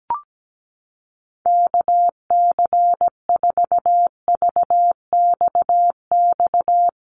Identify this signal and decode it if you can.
Morse code, heard as a keyed tone, about 17 wpm, 700 Hz. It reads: KC4VXX
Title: USA Call Signs Repeated 2x 17wpm